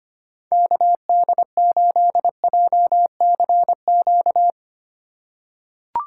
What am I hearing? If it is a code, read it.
Morse code, heard as a keyed tone, about 25 wpm, 700 Hz. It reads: KD8JCQ
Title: USA Call Signs - 6 Characters - 4x Character Spacing - 25wpm